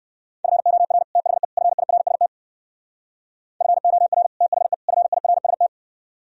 Morse code, heard as a keyed tone, about 58 wpm, 700 Hz. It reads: FOR THE FIRST FOR THE FIRST